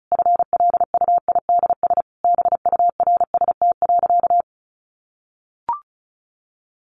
Morse code, heard as a keyed tone, about 34 wpm, 700 Hz. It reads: FLUIDS BURST.
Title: A Sound Of Thunder 35wpm